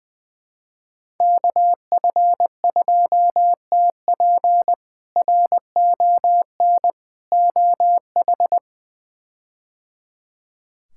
Morse code, heard as a keyed tone, about 21 wpm, 700 Hz. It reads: KF2TP RON OH
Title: K1USN SST QSO Practice - 20wpm